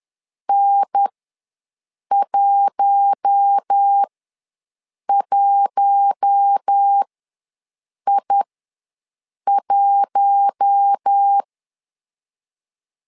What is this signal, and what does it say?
Morse code, heard as a keyed tone, about 10 wpm, spaced as for 6 wpm, 800 Hz. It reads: N11I1
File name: LEZIONI / 4 Q1 ws 10wpm 2538